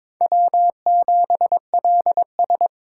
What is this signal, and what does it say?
Morse code, heard as a keyed tone, about 21 wpm, 700 Hz. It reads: W7LS
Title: USA Call Signs - 4x Character Spacing - 22wpm